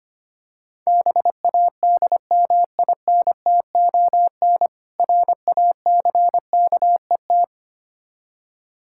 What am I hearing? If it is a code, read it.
Morse code, heard as a keyed tone, about 25 wpm, 700 Hz. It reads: BADMINTON RACKET
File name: practice / 25wpm / Difficult Words with Hints - 25wpm